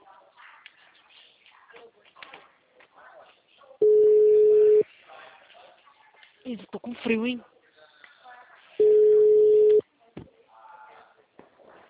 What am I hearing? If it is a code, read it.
Ringback tone (Europe, 425 Hz), with speech.